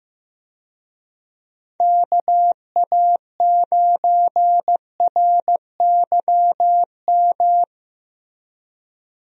Morse code, heard as a keyed tone, about 15 wpm, 700 Hz. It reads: KA9RYM
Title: USA Call Signs - 6 Characters - Repeated 3x - 15wpm